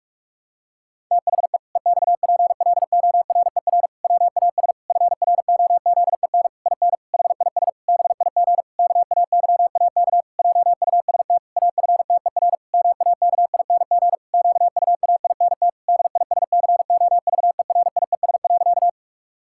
Morse code, heard as a keyed tone, about 46 wpm, 700 Hz. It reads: THE EXPLORER WAS FROZEN IN HIS BIG KAYAK JUST AFTER MAKING QUAINT DISCOVERIES.
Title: Pangram Challenge v2 - 45wpm